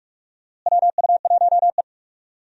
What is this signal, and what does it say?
Morse code, heard as a keyed tone, about 46 wpm, 700 Hz. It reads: WU1E
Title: USA Call Signs - 4 Characters - Repeated 3x - 45wpm